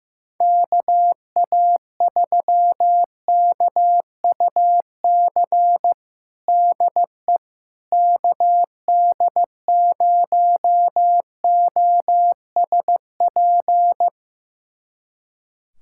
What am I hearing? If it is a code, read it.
Morse code, heard as a keyed tone, about 15 wpm, 700 Hz. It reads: KA3KUC DE KD0OSP
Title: USA Callsign DE Callsign 15wpm